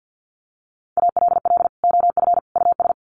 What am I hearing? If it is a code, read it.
Morse code, heard as a keyed tone, about 50 wpm, 700 Hz. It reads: ALL OF US
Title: Sets of 3 Words 50wpm